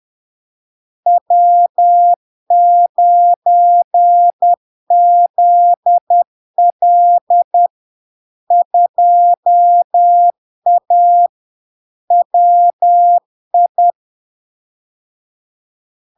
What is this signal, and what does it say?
Morse code, heard as a keyed tone, about 10 wpm, 700 Hz. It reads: W9ZL 2A WI
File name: LARK-FD-2022-Log-in-CW-10wpm-700-Hz